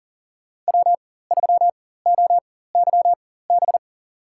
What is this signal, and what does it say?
Morse code, heard as a keyed tone, about 40 wpm, spaced as for 19 wpm, 700 Hz. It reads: W3OYB
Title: USA Call Signs - 5 Characters - 4x Character Spacing - 40wpm